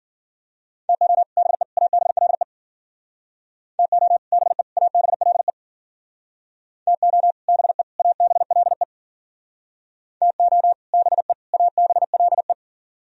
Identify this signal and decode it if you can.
Morse code, heard as a keyed tone, about 53 wpm, 700 Hz. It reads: TO BE ABLE TO BE ABLE TO BE ABLE TO BE ABLE